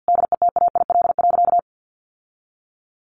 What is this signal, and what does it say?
Morse code, heard as a keyed tone, about 50 wpm, 700 Hz. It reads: DETAIL.
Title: Sentences from Top 1500 Words 50wpm